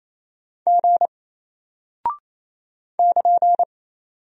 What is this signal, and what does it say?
Morse code, heard as a keyed tone, about 27 wpm, 700 Hz. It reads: G <KN>
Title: Single Character (Letter:Number:Punctuation:Prosign) Alt 28wpm